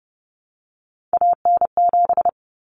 Morse code, heard as a keyed tone, about 30 wpm, 700 Hz. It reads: AN7
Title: Common Cut Numbers - 3-Digit 30wpm